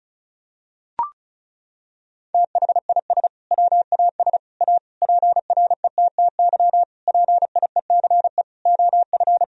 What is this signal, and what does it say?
Morse code, heard as a keyed tone, about 34 wpm, 700 Hz. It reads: THIS WAS A PRETTY PIECE OF
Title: Sentences from Top 1000 Words - Review Part 2 - 35wpm